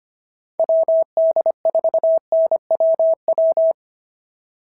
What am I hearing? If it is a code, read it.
Morse code, heard as a keyed tone, about 25 wpm, 690 Hz. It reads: WD4NWW
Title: SOTA QSO Practice 25wpm